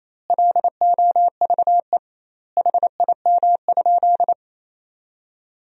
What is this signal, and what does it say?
Morse code, heard as a keyed tone, about 28 wpm, 700 Hz. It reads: LOVE HIM?
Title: 2x Word-Spacing - Sentences from Top 400 Words - 28wpm